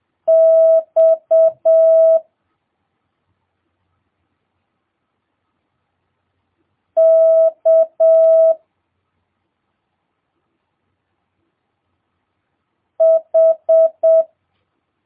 Morse code, heard as a keyed tone, about 7 wpm, 650 Hz. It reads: X K H
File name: jh8cbh / 2022cw